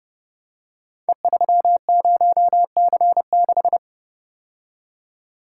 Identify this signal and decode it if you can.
Morse code, heard as a keyed tone, about 30 wpm, 700 Hz. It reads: E30C6
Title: Code Groups - 5 Letter:Numbers - 30wpm